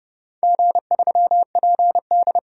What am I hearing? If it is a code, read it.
Morse code, heard as a keyed tone, about 30 wpm, 700 Hz. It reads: G3PD
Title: DXCC-100 Call Signs - 4 Characters - 30wpm